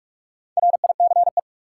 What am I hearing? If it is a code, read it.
Morse code, heard as a keyed tone, about 44 wpm, 700 Hz. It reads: RICE
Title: Four Letter Words - 45wpm